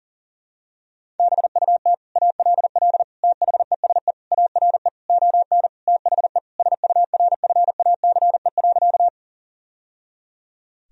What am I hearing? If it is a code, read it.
Morse code, heard as a keyed tone, about 40 wpm, 700 Hz. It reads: BUT ALL THESE ARE ON THE SURFACE.
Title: Sentences from Top 1000 Words - Review Part 3 - 40wpm